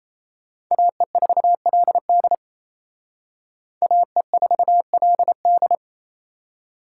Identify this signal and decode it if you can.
Morse code, heard as a keyed tone, about 36 wpm, 700 Hz. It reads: AE4LD AE4LD